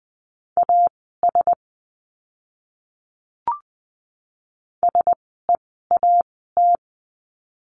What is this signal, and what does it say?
Morse code, heard as a keyed tone, about 20 wpm, spaced as for 15 wpm, 700 Hz. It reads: AS SEAT
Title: 061 Lesson 11f - Letter h - Words 20wpm